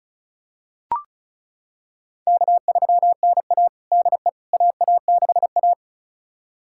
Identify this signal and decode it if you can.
Morse code, heard as a keyed tone, about 34 wpm, 700 Hz. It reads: K3NA DE AA6A
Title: USA Callsign DE Callsign - 4 Characters - 35wpm